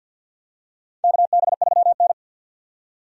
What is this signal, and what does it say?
Morse code, heard as a keyed tone, about 48 wpm, 700 Hz. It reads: KB3N
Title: USA Call Signs - 4 Characters - 50wpm